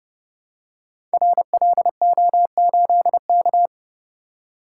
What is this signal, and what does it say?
Morse code, heard as a keyed tone, about 30 wpm, 700 Hz. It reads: RLO8K
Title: Code Groups - 5 Letter:Numbers - 30wpm